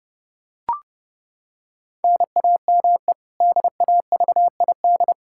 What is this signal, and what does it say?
Morse code, heard as a keyed tone, about 30 wpm, 700 Hz. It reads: NAME DAVID
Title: K1USN SST QSO Practice 30wpm